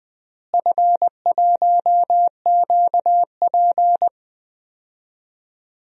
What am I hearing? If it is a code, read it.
Morse code, heard as a keyed tone, about 20 wpm, 700 Hz. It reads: F1QP
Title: DXCC-100 Call Signs - 4 Characters - Repeated 2x - 20wpm